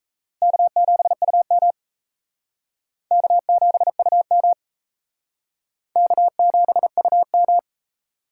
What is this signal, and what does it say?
Morse code, heard as a keyed tone, about 35 wpm, 700 Hz. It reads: K7UM K7UM K7UM
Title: Speed Racing - USA Call Signs 28wpm